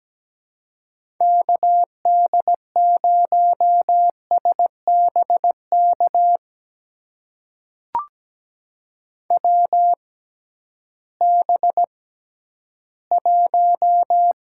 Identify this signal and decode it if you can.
Morse code, heard as a keyed tone, about 17 wpm, spaced as for 11 wpm, 700 Hz. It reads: KD0SBK WB1
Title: USA Call Signs - 6 Characters - 6x Character Spacing - 17wpm